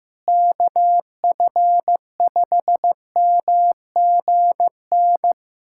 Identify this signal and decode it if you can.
Morse code, heard as a keyed tone, about 15 wpm, 700 Hz. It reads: KF5MGN
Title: USA Callsign DE Callsign - 6 Characters - 15wpm